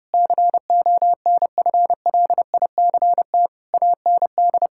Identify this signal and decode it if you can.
Morse code, heard as a keyed tone, about 32 wpm, 700 Hz. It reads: CONFLICT AND
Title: Sentences from Top 1600 Words 30wpm